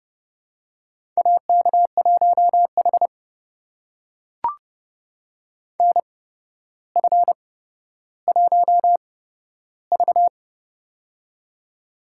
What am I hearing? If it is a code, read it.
Morse code, heard as a keyed tone, about 30 wpm, spaced as for 13 wpm, 700 Hz. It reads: AK1H NF1V
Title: USA Call Signs - 4 Characters - 8x Character Spacing - 30wpm